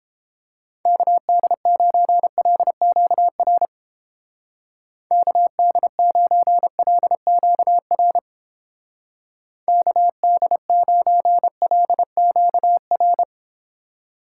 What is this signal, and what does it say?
Morse code, heard as a keyed tone, about 29 wpm, 700 Hz. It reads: KD9LQR KD9LQR KD9LQR